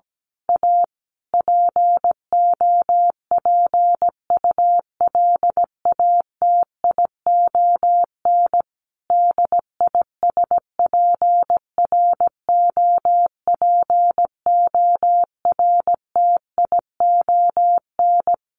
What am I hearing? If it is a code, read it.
Morse code, heard as a keyed tone, about 17 wpm, 700 Hz. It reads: A POPULATION DISPROPORTION
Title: A Sound Of Thunder 17wpm